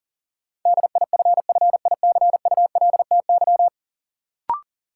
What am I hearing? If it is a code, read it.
Morse code, heard as a keyed tone, about 40 wpm, 700 Hz. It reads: DIFFICULTY
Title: Sentences from Top 1500 Words - New Word Drill Alt - 40wpm